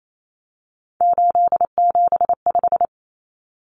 Morse code, heard as a keyed tone, about 28 wpm, 700 Hz. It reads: 875
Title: Numbers - 3-Digit 28wpm